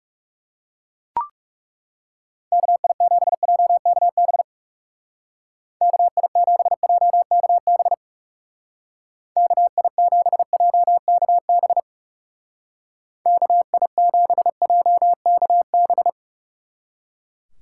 Morse code, heard as a keyed tone, about 36 wpm, 700 Hz. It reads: KI7JKB KI7JKB KI7JKB KI7JKB
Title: Speed Racing - USA 2x3 Call Signs 30wpm